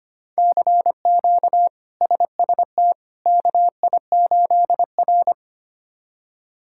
Morse code, heard as a keyed tone, about 25 wpm, 700 Hz. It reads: CQ SST KI8R
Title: K1USN SST QSO Practice - 25wpm